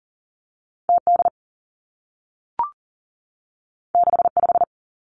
Morse code, heard as a keyed tone, about 39 wpm, 700 Hz. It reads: T D 6 5
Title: Two Challenging Letter-Number Combinations 40wpm